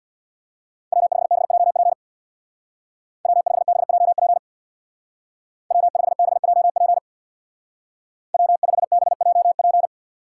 Morse code, heard as a keyed tone, about 65 wpm, 700 Hz. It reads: W5BJP W5BJP W5BJP W5BJP